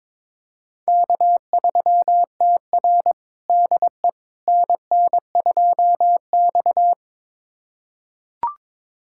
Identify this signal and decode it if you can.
Morse code, heard as a keyed tone, about 21 wpm, 700 Hz. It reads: K3TR DE NN2X
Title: USA Callsign DE Callsign - 4 Characters - 22wpm